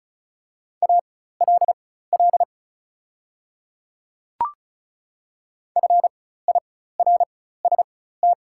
Morse code, heard as a keyed tone, about 36 wpm, spaced as for 16 wpm, 690 Hz. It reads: ALL FIRST
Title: 4x Character Spacing - Top 300 Words - 35wpm